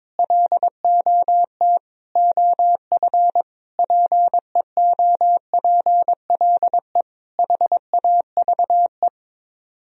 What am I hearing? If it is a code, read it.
Morse code, heard as a keyed tone, about 22 wpm, 700 Hz. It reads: LOT OF PEOPLE HAVE
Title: Speed-Racing - Sets of 5 Words 22wpm